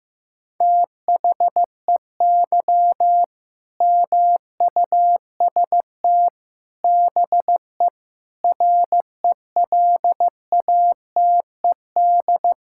Morse code, heard as a keyed tone, about 15 wpm, 700 Hz. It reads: THEY MUST BE RELATED
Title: Sentences from Top 1000 Words - Review Part 5 - 15wpm